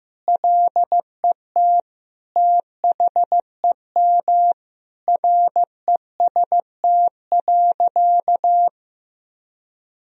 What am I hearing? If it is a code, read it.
Morse code, heard as a keyed tone, about 15 wpm, 700 Hz. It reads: LET THEM REST.
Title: Sentences from Top 1000 Words - Review Part 4 - 15wpm